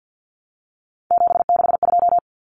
Morse code, heard as a keyed tone, about 50 wpm, 700 Hz. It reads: 762
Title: Common Cut Numbers - 3-Digit 50wpm